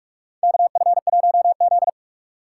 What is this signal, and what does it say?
Morse code, heard as a keyed tone, about 46 wpm, 700 Hz. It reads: KF1Z
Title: USA Call Signs - 4 Characters - 2x Character Spacing - 45wpm